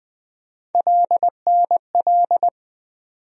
Morse code, heard as a keyed tone, about 20 wpm, 700 Hz. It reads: LNL
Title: USA Call Sign Postfixes - 3 Letters 20wpm